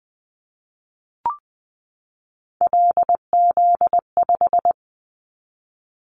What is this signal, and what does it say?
Morse code, heard as a keyed tone, about 20 wpm, 700 Hz. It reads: LZ5
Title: Callsign Prefixes for Top 25 DXCC - 20wpm